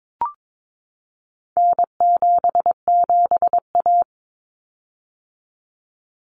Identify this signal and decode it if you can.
Morse code, heard as a keyed tone, about 22 wpm, 700 Hz. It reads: N77A
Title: Common Cut Numbers - 4-Digit 22wpm